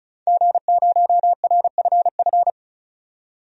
Morse code, heard as a keyed tone, about 34 wpm, 700 Hz. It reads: G0RFF
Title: DXCC-100 Call Signs - 5 Characters - 35wpm